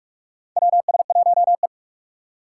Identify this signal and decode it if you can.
Morse code, heard as a keyed tone, about 43 wpm, 700 Hz. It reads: WS1E
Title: USA Call Signs - 4 Characters - 8x Character Spacing - 45wpm